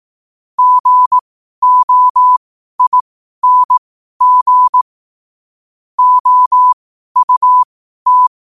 Morse code, heard as a keyed tone, about 18 wpm, spaced as for 12 wpm, 1000 Hz. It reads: GOING OUT
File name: qotdfiles / 12wpm